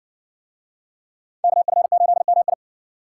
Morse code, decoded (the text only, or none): KF8MI